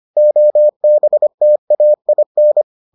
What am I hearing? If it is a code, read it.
Morse code, heard as a keyed tone, about 25 wpm, 600 Hz. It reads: OBTAIN